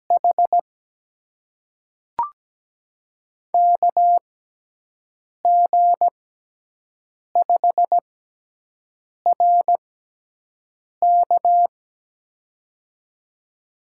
Morse code, heard as a keyed tone, about 17 wpm, spaced as for 6 wpm, 700 Hz. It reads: H KG5RK
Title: USA Call Signs - 5 Characters - 6x Character Spacing - 17wpm